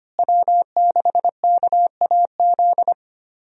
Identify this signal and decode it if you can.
Morse code, heard as a keyed tone, about 25 wpm, 700 Hz. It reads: W6KAZ
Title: USA Call Signs - 5 Characters - Repeated 3x - 25wpm